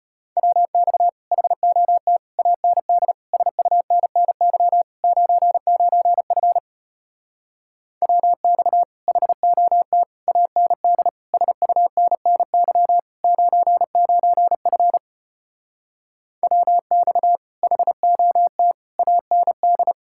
Morse code, heard as a keyed tone, about 33 wpm, 700 Hz. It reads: WX HOT AND SUNNY 99F WX HOT AND SUNNY 99F WX HOT AND